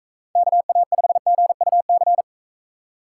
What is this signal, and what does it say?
Morse code, heard as a keyed tone, about 44 wpm, 700 Hz. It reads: KA5GUC